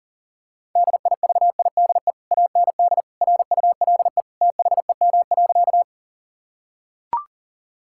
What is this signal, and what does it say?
Morse code, heard as a keyed tone, about 40 wpm, 700 Hz. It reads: DIVIDE AND RULE THEM.
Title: Sentences from Top 1600 Words 40wpm